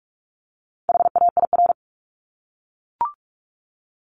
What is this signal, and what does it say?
Morse code, heard as a keyed tone, about 44 wpm, 700 Hz. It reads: HAIR
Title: Top 1751-2000 Words 45wpm